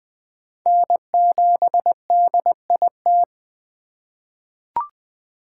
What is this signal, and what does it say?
Morse code, heard as a keyed tone, about 20 wpm, 700 Hz. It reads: N7DIT
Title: USA Call Signs - 5 Characters - 3x Character Spacing - 20wpm